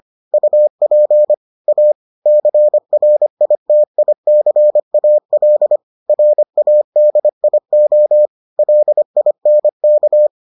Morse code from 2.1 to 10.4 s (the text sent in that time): CRITICAL RADIO LINK